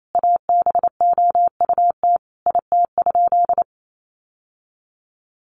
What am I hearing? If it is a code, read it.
Morse code, heard as a keyed tone, about 28 wpm, 700 Hz. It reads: ABOUT IT?
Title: Sentences from Top 1100 Words 28wpm